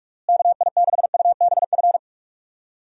Morse code, heard as a keyed tone, about 46 wpm, 700 Hz. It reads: KI6UBF
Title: USA Call Signs - 6 Characters - Repeated 2x - 45wpm